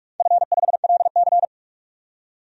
Morse code, heard as a keyed tone, about 46 wpm, 700 Hz. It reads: F5LC
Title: DXCC-100 Call Signs - 4 Characters - 45wpm